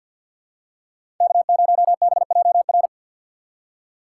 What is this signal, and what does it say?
Morse code, heard as a keyed tone, about 46 wpm, 700 Hz. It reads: K0BJR
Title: USA Call Signs - 5 Characters - 6x Character Spacing - 50wpm